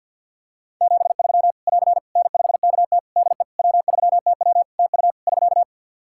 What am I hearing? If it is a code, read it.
Morse code, heard as a keyed tone, about 52 wpm, 700 Hz. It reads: 73 <AR> N5KT DE W3TW TU <SK>
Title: Practice QSO - Exchange 4 - Ending - 50wpm